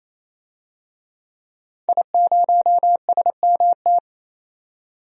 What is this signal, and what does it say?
Morse code, heard as a keyed tone, about 28 wpm, 700 Hz. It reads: I0SMT
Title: DXCC-100 Call Signs - 5 Characters - 28wpm